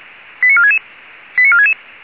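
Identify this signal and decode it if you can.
Five-tone selective call (ZVEI), heard as quick runs of five tones, tones of about 80 ms. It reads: 89466 89466
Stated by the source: ZVEI five-tone call received by SDR.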